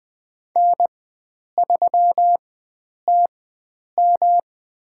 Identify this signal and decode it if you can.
Morse code, heard as a keyed tone, about 20 wpm, spaced as for 9 wpm, 700 Hz. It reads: N3TM
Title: USA Call Signs - 4 Characters - 4x Character Spacing - 20wpm